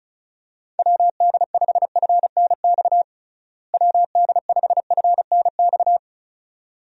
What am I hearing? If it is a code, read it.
Morse code, heard as a keyed tone, about 34 wpm, 700 Hz. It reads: WD5FNX WD5FNX
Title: USA Call Signs - 6 Characters - Repeated 2x - 35wpm